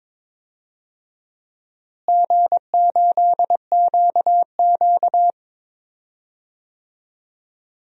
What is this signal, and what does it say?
Morse code, heard as a keyed tone, about 21 wpm, 700 Hz. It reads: G8QQ
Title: DXCC-100 Call Signs - 4 Characters - Repeated 2x - 22wpm